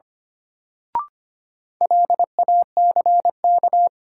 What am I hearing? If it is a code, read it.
Morse code, heard as a keyed tone, about 25 wpm, 700 Hz. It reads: LACK
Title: Sets of 3 Words - New Word Drill - 25wpm